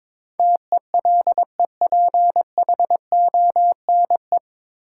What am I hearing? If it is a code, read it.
Morse code, heard as a keyed tone, about 22 wpm, 700 Hz. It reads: TELEPHONE
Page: Morse Code Ninja practice 22wpm